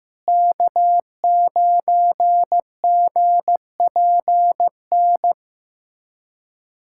Morse code, heard as a keyed tone, about 16 wpm, 700 Hz. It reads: K9GPN
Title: USA Callsign DE Callsign - 5 Characters - 15wpm